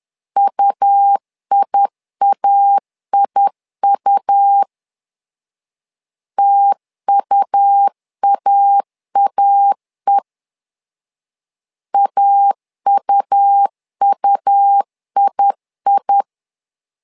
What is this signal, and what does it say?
Morse code, heard as a keyed tone, about 10 wpm, 800 Hz. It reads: UIAIU TUAAE AUUII
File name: LEZIONI / 2 UHDN / 2 U 10wpm 1622